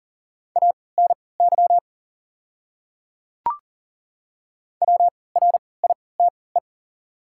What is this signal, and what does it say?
Morse code, heard as a keyed tone, about 40 wpm, spaced as for 23 wpm, 700 Hz. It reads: ANY WRITE
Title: 3x Character Spacing - Top 300 Words - 40wpm